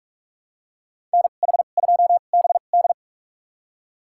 Morse code, heard as a keyed tone, about 46 wpm, spaced as for 33 wpm, 700 Hz. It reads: NH2BD
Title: USA Call Signs - 5 Characters - 2x Character Spacing - 45wpm